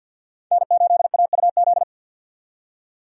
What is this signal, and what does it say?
Morse code, heard as a keyed tone, about 50 wpm, 700 Hz. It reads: N8AUZ
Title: USA Call Signs - 5 Characters - 50wpm